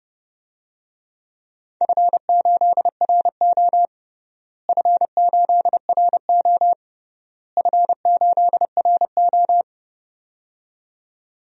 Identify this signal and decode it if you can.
Morse code, heard as a keyed tone, about 31 wpm, 700 Hz. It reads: F8RO F8RO F8RO
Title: DXCC-100 Call Signs - 4 Characters - Repeated 3x - 30wpm